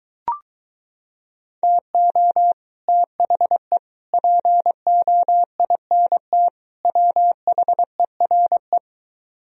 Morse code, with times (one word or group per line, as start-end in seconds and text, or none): 1.6-2.5 TO
2.9-3.8 THE
4.1-6.5 POINT
6.8-8.8 WHERE